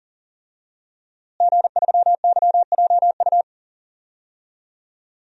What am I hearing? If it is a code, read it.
Morse code, heard as a keyed tone, about 37 wpm, 700 Hz. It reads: G3YJU
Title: DXCC-100 Call Signs - 5 Characters - Repeated 3x - 40wpm